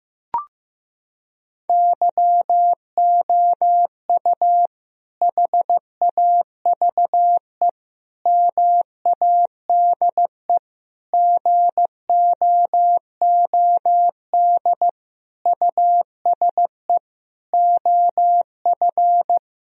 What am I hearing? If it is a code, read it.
Morse code, heard as a keyed tone, about 15 wpm, 700 Hz. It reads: YOU HAVE MADE GOOD USE OF
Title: Sentences from Top 1000 Words - Review Part 1 - 15wpm